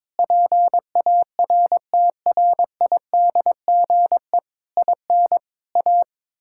Morse code, heard as a keyed tone, about 22 wpm, 700 Hz. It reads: PARTRIDGE IN A
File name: Sentences with 9-Word Phrases - 22wpm